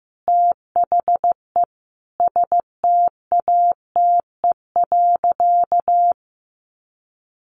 Morse code, heard as a keyed tone, about 15 wpm, 700 Hz. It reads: THE STATE.
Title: Sentences from Top 1300 Words 15wpm